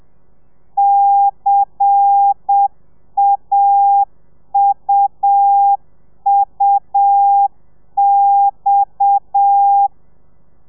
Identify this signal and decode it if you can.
Morse code, heard as a keyed tone, about 7 wpm, 800 Hz. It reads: CAUUX
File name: telegrafia / i2tfj / Lez10 30